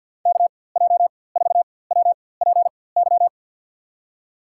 Morse code, heard as a keyed tone, about 52 wpm, spaced as for 23 wpm, 700 Hz. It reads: KJ4WPY
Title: USA Call Signs - 6 Characters - 4x Character Spacing - 50wpm